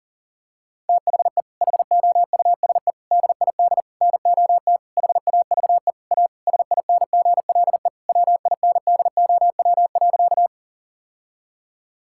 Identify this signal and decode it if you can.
Morse code, heard as a keyed tone, about 40 wpm, 700 Hz. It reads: THE HOUSE DID NOT HAVE A SINGLE WINDOW.
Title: Sentences from Top 1600 Words 40wpm